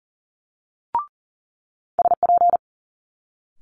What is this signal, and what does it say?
Morse code, heard as a keyed tone, about 41 wpm, 700 Hz. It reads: SP
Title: Multi-letter Phonemes Quick Fire 40wpm